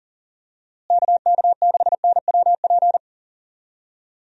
Morse code, heard as a keyed tone, about 40 wpm, 700 Hz. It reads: KK6NWP
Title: USA Call Signs - 6 Characters - Repeated 2x - 40wpm